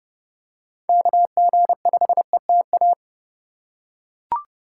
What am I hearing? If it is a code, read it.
Morse code, heard as a keyed tone, about 28 wpm, 700 Hz. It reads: KG5ETA